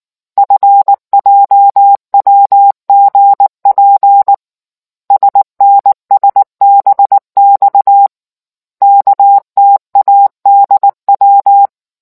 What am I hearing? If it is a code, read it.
Morse code, heard as a keyed tone, about 19 wpm, 800 Hz. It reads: FJWGP SNSBX KTADW